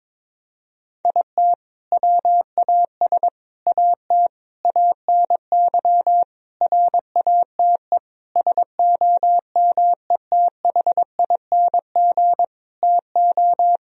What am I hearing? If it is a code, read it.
Morse code, heard as a keyed tone, about 22 wpm, 700 Hz. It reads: IT WAS AT ANY RATE SOMETHING TO
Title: Sentences from Top 1000 Words - Review Part 4 - 22wpm